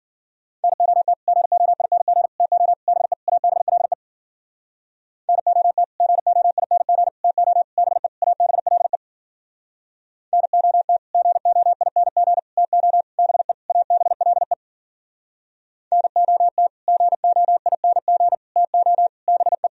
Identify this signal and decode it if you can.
Morse code, heard as a keyed tone, about 53 wpm, 700 Hz. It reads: NOT GOING TO BE ABLE NOT GOING TO BE ABLE NOT GOING TO BE ABLE NOT GOING TO BE